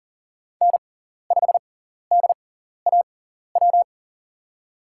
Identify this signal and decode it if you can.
Morse code, heard as a keyed tone, about 38 wpm, spaced as for 14 wpm, 700 Hz. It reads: N5DAW
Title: USA Call Signs - 5 Characters - 6x Character Spacing - 40wpm